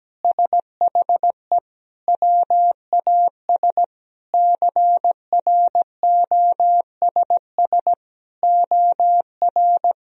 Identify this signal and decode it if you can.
Morse code, heard as a keyed tone, about 17 wpm, 700 Hz. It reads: SHE WAS CROSS OR
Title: Sentences from Top 1000 Words - Review Part 3 - 17wpm